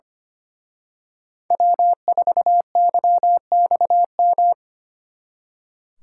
Morse code, heard as a keyed tone, about 31 wpm, 700 Hz. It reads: W4YXM
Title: Speed Racing - USA 1x3 Call Signs 25wpm